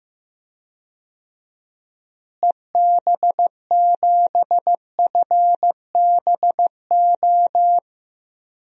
Morse code, heard as a keyed tone, about 16 wpm, 700 Hz. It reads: EB7FBO
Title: DXCC-100 Call Signs - 6 Characters - Repeated 2x - 15wpm